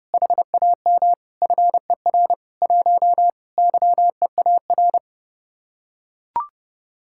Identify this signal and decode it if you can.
Morse code, heard as a keyed tone, about 30 wpm, 700 Hz. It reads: HAM FER 1 YEAR